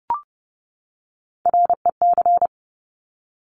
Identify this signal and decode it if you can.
Morse code, heard as a keyed tone, about 30 wpm, 690 Hz. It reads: REC
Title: Top Three Letter Combinations in Words 30wpm